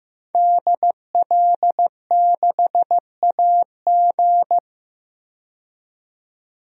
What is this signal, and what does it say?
Morse code, heard as a keyed tone, about 15 wpm, 700 Hz. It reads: DL6AG
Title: DXCC-100 Call Signs - 5 Characters - 15wpm